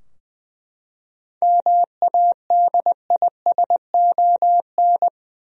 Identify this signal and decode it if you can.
Morse code, heard as a keyed tone, about 20 wpm, 700 Hz. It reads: MADISON
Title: USA Common Names - 20wpm